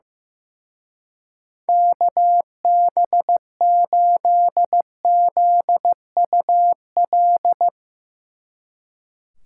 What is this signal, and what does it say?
Morse code, heard as a keyed tone, about 16 wpm, 700 Hz. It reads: KB8ZUL
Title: Speed Racing - USA 2x3 Call Signs 15wpm